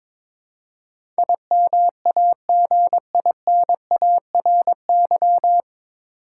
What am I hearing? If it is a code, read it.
Morse code, heard as a keyed tone, about 22 wpm, 700 Hz. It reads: IMAGINARY
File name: 9 Letter Words - First 3 Letters Given - 22wpm